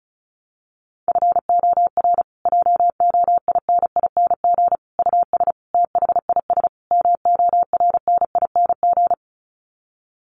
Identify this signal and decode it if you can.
Morse code, heard as a keyed tone, about 34 wpm, 700 Hz. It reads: FOR JOINING US THIS MORNING
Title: Sets of 5 Words Encore 35wpm